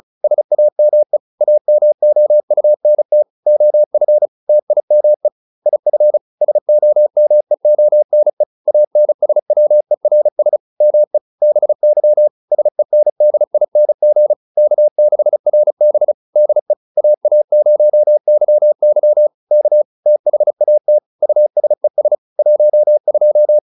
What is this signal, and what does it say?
Morse code, heard as a keyed tone, about 36 wpm, 600 Hz. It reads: SAME AMOUNT OF TIME IF SOMEONE ANSWERS ME BY SENDING K6RB DE AA0YY K THAT USES 12